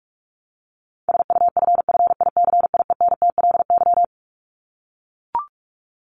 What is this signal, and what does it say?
Morse code, heard as a keyed tone, about 44 wpm, 700 Hz. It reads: SUFFICIENTLY
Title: LY Suffix - Top 100 Words with Base Words 45wpm